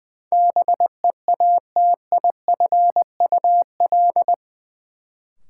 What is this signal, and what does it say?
Morse code, heard as a keyed tone, about 20 wpm, 700 Hz. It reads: BEATIFUL
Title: Sentences from Top 1300 Words - New Word Drill Alt - 20wpm